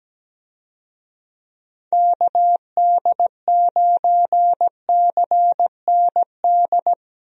Morse code, heard as a keyed tone, about 17 wpm, 700 Hz. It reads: KD9CND
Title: USA Call Signs - 6 Characters - Repeated 2x - 17wpm